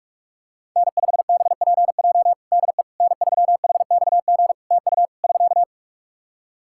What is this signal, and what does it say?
Morse code, heard as a keyed tone, about 43 wpm, 700 Hz. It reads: N5BPJ DE N3HXG TU <SK>
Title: K1USN SST QSO Practice 45wpm